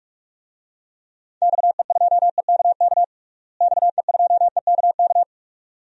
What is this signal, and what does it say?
Morse code, heard as a keyed tone, about 46 wpm, 700 Hz. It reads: XE2EKK XE2EKK
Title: DXCC-100 Call Signs - 6 Characters - Repeated 2x - 45wpm